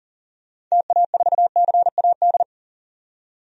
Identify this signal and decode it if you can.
Morse code, heard as a keyed tone, about 40 wpm, 700 Hz. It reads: TA4CAD
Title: DXCC-100 Call Signs - 6 Characters - 40wpm